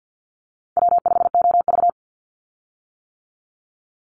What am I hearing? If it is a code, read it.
Morse code, heard as a keyed tone, about 50 wpm, 700 Hz. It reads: W5OV
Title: Call Signs - 50wpm